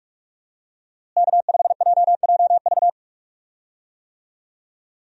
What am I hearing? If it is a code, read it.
Morse code, heard as a keyed tone, about 41 wpm, 700 Hz. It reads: K5JJV
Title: USA Call Signs - 5 Characters - Repeated 2x - 45wpm